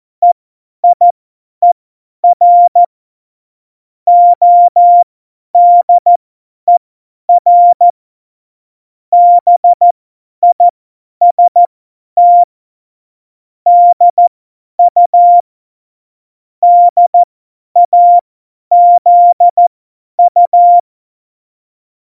Morse code, heard as a keyed tone, about 14 wpm, spaced as for 10 wpm, 700 Hz. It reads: EIER ODER BIST DU DAZU
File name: Morsekurs / Witze / Witz 20. August 2019, 50 F70